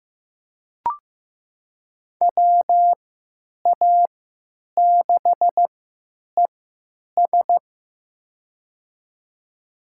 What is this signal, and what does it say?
Morse code, heard as a keyed tone, about 15 wpm, spaced as for 9 wpm, 700 Hz. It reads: WA6ES
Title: USA Call Signs - 3x Character Spacing - 15wpm